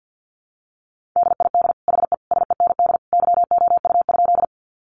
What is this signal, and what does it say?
Morse code, heard as a keyed tone, about 48 wpm, 700 Hz. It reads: DID HE SEND YOU?
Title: Sentences from Top 1500 Words 50wpm